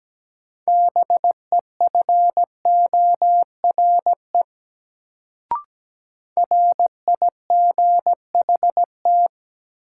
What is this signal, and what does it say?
Morse code, heard as a keyed tone, about 17 wpm, 700 Hz. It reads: BEFORE RIGHT
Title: Sets of 2 Words - New Word Drill - 17wpm